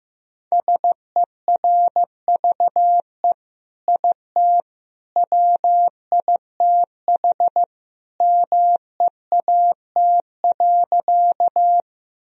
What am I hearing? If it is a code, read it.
Morse code, heard as a keyed tone, about 15 wpm, 700 Hz. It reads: SERVE IT WITH MEAT.
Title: Sentences from Top 1000 Words - Review Part 3 - 15wpm